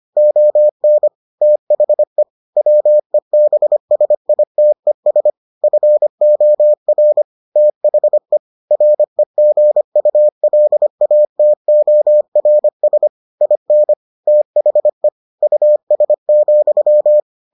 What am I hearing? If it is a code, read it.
Morse code, heard as a keyed tone, about 25 wpm, 600 Hz. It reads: ON THE WEBSITES FOR THE REGULATORS IN THE US,